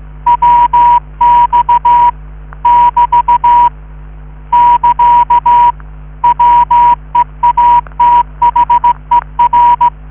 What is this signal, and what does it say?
Morse code, heard as a keyed tone, about 16 wpm, 960 Hz. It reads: WX <BT> <KA> WEATHER